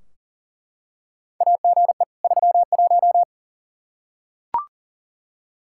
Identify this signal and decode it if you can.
Morse code, heard as a keyed tone, about 40 wpm, 700 Hz. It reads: AGE 31
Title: K1USN SST QSO Practice 40wpm